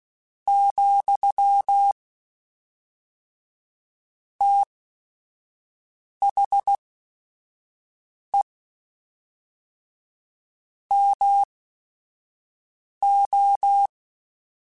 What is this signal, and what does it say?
Morse code, heard as a keyed tone, about 16 wpm, spaced as for 5 wpm, 800 Hz. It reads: , THE MO
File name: Technical Subjects / Technical Subjects-mp3s / XC @5